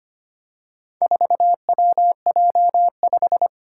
Morse code, heard as a keyed tone, about 25 wpm, 700 Hz. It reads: 4WJ5
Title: Four Challenging Letter-Number Combinations 25wpm